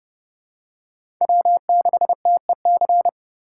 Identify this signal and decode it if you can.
Morse code, heard as a keyed tone, about 30 wpm, 700 Hz. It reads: W6TEC
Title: USA Call Signs - 5 Characters - 3x Character Spacing - 30wpm